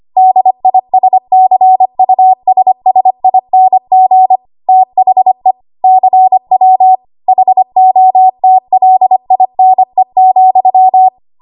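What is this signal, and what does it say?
Morse code, heard as a keyed tone, about 25 wpm, 750 Hz. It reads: DISCUSSING THE CW HOTLINE,
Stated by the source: ARRL 25 WPM code practice archive.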